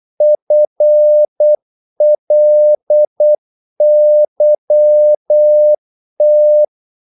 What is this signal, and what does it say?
Morse code, heard as a keyed tone, about 8 wpm, 600 Hz. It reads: FLYT
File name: Korte ord DK 8wpm